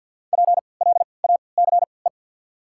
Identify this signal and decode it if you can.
Morse code, heard as a keyed tone, about 48 wpm, spaced as for 28 wpm, 690 Hz. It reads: PLACE
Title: 3x Character Spacing - Top 300 Words - 50wpm